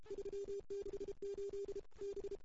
Morse code, heard as a keyed tone, about 35 wpm, 410 Hz. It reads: 368 B
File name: CW / 2008-ss-cw-k5tr-op-n5rz -00:00